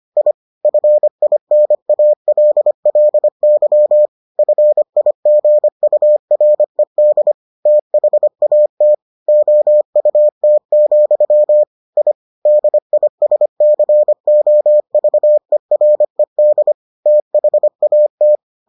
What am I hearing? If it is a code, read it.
Morse code, heard as a keyed tone, about 25 wpm, 600 Hz. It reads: I FINALLY FIGURED THAT OUT, I DISCOVERED THAT